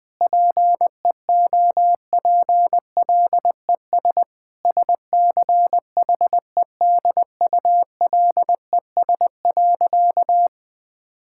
Morse code, heard as a keyed tone, about 20 wpm, 700 Hz. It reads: PEOPLES SCHEDULES.
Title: Difficult Sentences with Hints - 20wpm